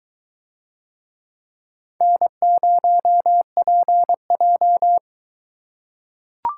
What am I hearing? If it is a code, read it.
Morse code, heard as a keyed tone, about 21 wpm, 700 Hz. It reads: N0PJ